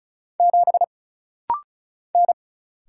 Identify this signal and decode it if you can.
Morse code, heard as a keyed tone, about 36 wpm, 700 Hz. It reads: 7 N
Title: Single Letter-Number Alt - Rapid-Fire 35wpm